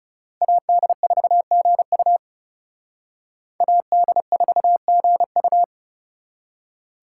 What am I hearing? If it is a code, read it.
Morse code, heard as a keyed tone, about 36 wpm, 700 Hz. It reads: AD4GU AD4GU
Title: Speed Racing - USA 2x2 Call Signs 30wpm